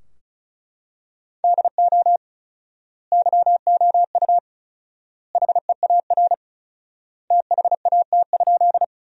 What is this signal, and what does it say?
Morse code, heard as a keyed tone, about 36 wpm, 700 Hz. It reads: DO YOU HEAR THAT?
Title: 4x Word-Spacing - Sentences from Top 300 Words - 35wpm